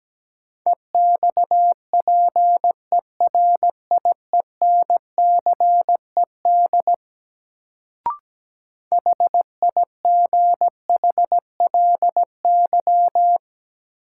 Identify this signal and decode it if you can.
Morse code, heard as a keyed tone, about 17 wpm, 700 Hz. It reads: EXPERIENCED HIGHLY
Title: Sentences from Top 1100 Words - New Word Drill - 17wpm